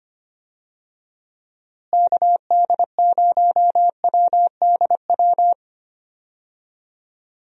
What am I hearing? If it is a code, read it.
Morse code, heard as a keyed tone, about 25 wpm, 700 Hz. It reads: KD0WDW
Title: USA Call Signs - 6 Characters - Repeated 3x - 25wpm